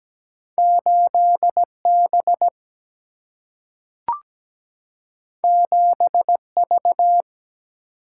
Morse code, heard as a keyed tone, about 17 wpm, 700 Hz. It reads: 8 B 7 V
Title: Single-Letter Single-Number 17wpm